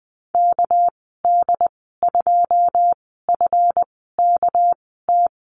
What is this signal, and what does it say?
Morse code, heard as a keyed tone, about 20 wpm, spaced as for 14 wpm, 700 Hz. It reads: KD2FKT
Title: 207 Lesson 33g - Letter j - Call Signs 20wpm